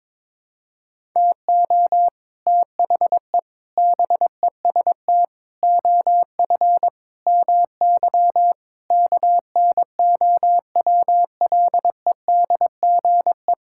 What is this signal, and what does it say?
Morse code, heard as a keyed tone, about 22 wpm, 700 Hz. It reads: TO THE BEST OF MY KNOWLEDGE
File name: practice / 22wpm / Sets of 6 Words - 22wpm